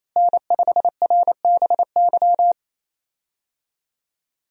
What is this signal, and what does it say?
Morse code, heard as a keyed tone, about 28 wpm, 700 Hz. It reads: N5RBY
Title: USA Call Signs - 5 Characters - Repeated 3x - 28wpm